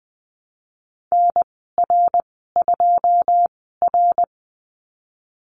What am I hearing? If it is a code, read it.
Morse code, heard as a keyed tone, about 20 wpm, spaced as for 15 wpm, 700 Hz. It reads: NR2R
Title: 082 Lesson 14g - Number 2 - Call Signs 20wpm